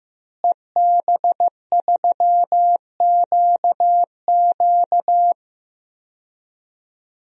Morse code, heard as a keyed tone, about 15 wpm, 700 Hz. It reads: EB3QQ
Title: DXCC-100 Call Signs - 5 Characters - Repeated 2x - 15wpm